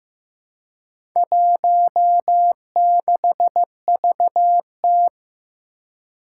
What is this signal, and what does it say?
Morse code, heard as a keyed tone, about 15 wpm, 700 Hz. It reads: 16VT
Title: Four Challenging Letter-Number Combinations 15wpm